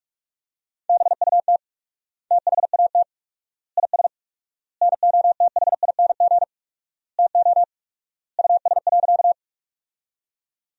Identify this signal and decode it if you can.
Morse code, heard as a keyed tone, about 46 wpm, 690 Hz. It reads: BUT THAT IS NOTHING TO US.
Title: 4x Word-Spacing - Sentences from Top 500 Words - 45wpm